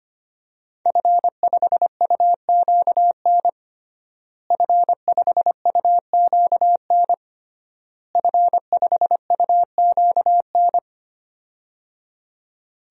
Morse code, heard as a keyed tone, about 25 wpm, 700 Hz. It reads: F5UQN F5UQN F5UQN
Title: DXCC-100 Call Signs - 5 Characters - Repeated 3x - 25wpm